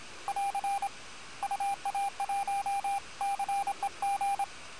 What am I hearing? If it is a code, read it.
Morse code, heard as a keyed tone, about 26 wpm, 800 Hz, with radio noise. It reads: <AR> UA1CEG
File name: attachments / QSO HA9PP